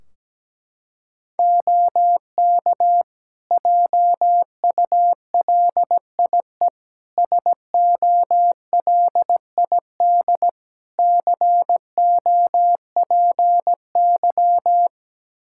Morse code, heard as a keyed tone, about 17 wpm, 700 Hz. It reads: OK JULIE SOLID COPY
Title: Practice QSO - Exchange 3 - More Info - 17wpm